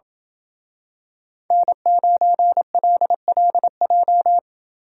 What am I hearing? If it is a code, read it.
Morse code, heard as a keyed tone, about 27 wpm, 700 Hz. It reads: N9LLJ